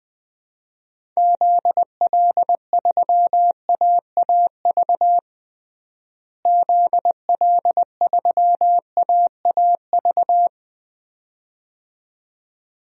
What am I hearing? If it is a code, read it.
Morse code, heard as a keyed tone, about 20 wpm, 690 Hz. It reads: ZL3AAV ZL3AAV
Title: DXCC-100 Call Signs - 6 Characters - Repeated 2x - 20wpm